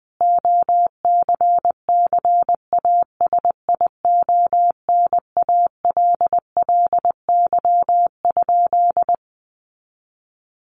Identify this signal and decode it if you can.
Morse code, heard as a keyed tone, about 20 wpm, 700 Hz. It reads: OCCASIONALLY?
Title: Sentences from Top 1300 Words 20wpm